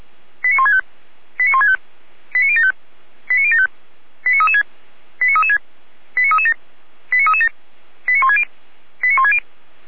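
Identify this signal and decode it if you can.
Five-tone selective call (ZVEI), heard as quick runs of five tones, tones of about 70 ms. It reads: 89165 89165 89075 89075 89336 89336 89337 89337 89160 89160
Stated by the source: ZVEI five-tone selective call received by SDR, ID 89078, channel 455.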